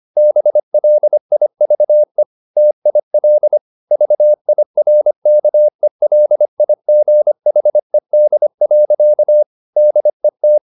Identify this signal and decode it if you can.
Morse code, heard as a keyed tone, about 25 wpm, 600 Hz. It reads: BLIVE TIL VIRKELIGHED. DET
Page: Ferie 25 wpm